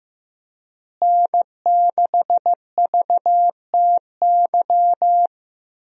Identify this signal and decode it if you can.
Morse code, heard as a keyed tone, about 16 wpm, 700 Hz. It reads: N6VTY
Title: USA Call Signs - 5 Characters - Repeated 3x - 15wpm